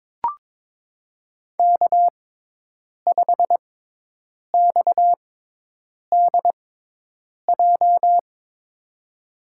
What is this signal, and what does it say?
Morse code, heard as a keyed tone, about 22 wpm, spaced as for 8 wpm, 700 Hz. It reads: K5XDJ
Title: USA Call Signs - 5 Characters - 6x Character Spacing - 22wpm